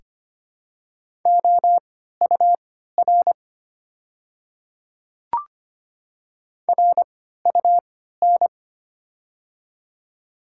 Morse code, heard as a keyed tone, about 25 wpm, spaced as for 14 wpm, 700 Hz. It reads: OUR RUN
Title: 3x Character Spacing - Top 300 Words - 25wpm